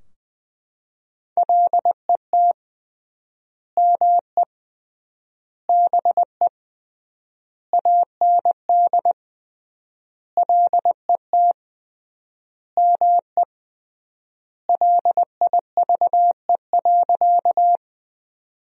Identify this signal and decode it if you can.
Morse code, heard as a keyed tone, about 21 wpm, 700 Hz. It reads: LET ME BE AND LET ME LIVE.
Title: 3x Word-Spacing - Sentences from Top 300 Words - 20wpm